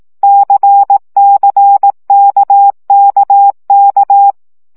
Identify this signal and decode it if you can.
Morse code, heard as a keyed tone, about 18 wpm, 800 Hz. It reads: CCKKK
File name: CW gyakorlas k c y 15 baud